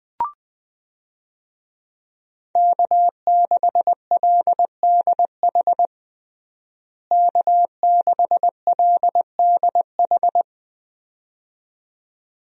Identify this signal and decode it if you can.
Morse code, heard as a keyed tone, about 21 wpm, 700 Hz. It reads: K6LDH K6LDH
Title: USA Call Signs - 5 Characters - Repeated 2x - 20wpm